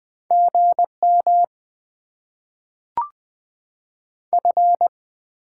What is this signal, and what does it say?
Morse code, heard as a keyed tone, about 20 wpm, 700 Hz. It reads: GM F
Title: Top 20 QSO Elements - 6x Character Spacing - 20wpm